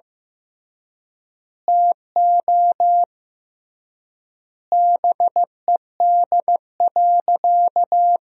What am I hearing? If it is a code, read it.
Morse code, heard as a keyed tone, about 16 wpm, 700 Hz. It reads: TO BED.
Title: 3x Word-Spacing - Sentences from Top 500 Words - 15wpm